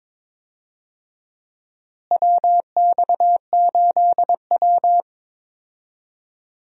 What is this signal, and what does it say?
Morse code, heard as a keyed tone, about 22 wpm, 700 Hz. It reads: WX8W
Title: USA Call Signs - 4 Characters - Repeated 3x - 22wpm